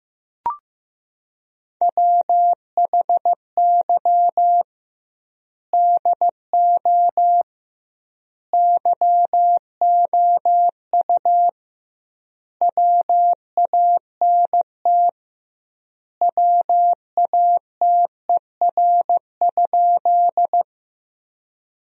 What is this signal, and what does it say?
Morse code, heard as a keyed tone, about 15 wpm, 700 Hz. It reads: WHY DO YOU WANT WATER?
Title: 2x Word-Spacing - Sentences from Top 200 Words - 15wpm